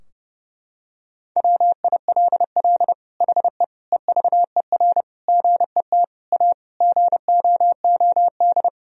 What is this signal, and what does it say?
Morse code, heard as a keyed tone, about 30 wpm, 700 Hz. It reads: WILL HE EVER GET A GOOD
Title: Sentences from Top 1000 Words - Review Part 3 - 30wpm